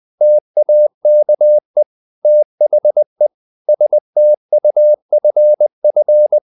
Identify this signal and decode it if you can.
Morse code, heard as a keyed tone, about 20 wpm, 600 Hz. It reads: TAKE THE STUFF
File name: the old man and the sea / the old man and the sea part 1 20wpm